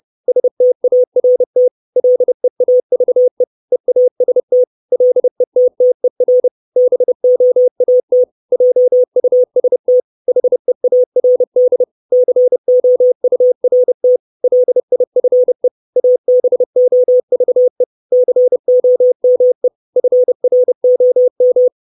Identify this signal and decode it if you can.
Morse code, heard as a keyed tone, about 30 wpm, 500 Hz. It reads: START LEAVE EAST LETTER BOAT JUST HEARD COURT LIFE ABOVE COME FROM
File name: 632 Words 30WPM